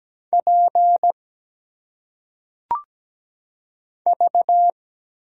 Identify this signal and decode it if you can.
Morse code, heard as a keyed tone, about 17 wpm, 700 Hz. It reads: P V
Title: Single Letters - Phonetic Pronunciation 17wpm